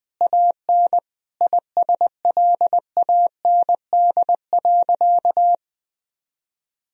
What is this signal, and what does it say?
Morse code, heard as a keyed tone, about 21 wpm, 700 Hz. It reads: AN ISLAND.
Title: Sentences from Top 1000 Words - Review Part 3 - 20wpm